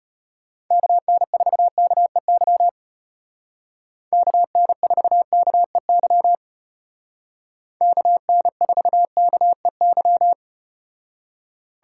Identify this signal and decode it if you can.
Morse code, heard as a keyed tone, about 34 wpm, 700 Hz. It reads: KN4KEY KN4KEY KN4KEY